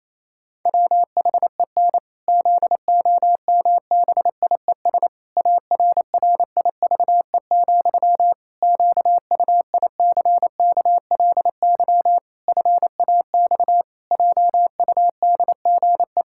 Morse code, heard as a keyed tone, about 28 wpm, 700 Hz. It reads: WHEN ZOMBIES ARRIVE, QUICKLY FAX JUDGE
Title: Pangram Challenge v2 - 28wpm